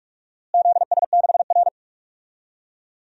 Morse code, heard as a keyed tone, about 44 wpm, 700 Hz. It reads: ZS6R
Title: DXCC-100 Call Signs - 4 Characters - Repeated 2x - 45wpm